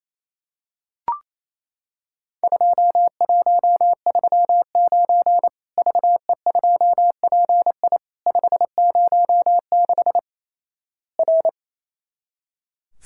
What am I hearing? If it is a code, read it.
Morse code, heard as a keyed tone, about 28 wpm, 700 Hz. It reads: 2139 VE2PI 506 R
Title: Worked All Europe DX Contest - QTC Exchange - 28wpm